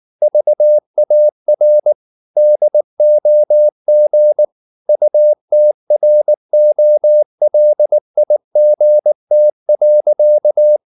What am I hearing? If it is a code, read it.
Morse code, heard as a keyed tone, about 19 wpm, 600 Hz. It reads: VAR DOG UTROLIGT.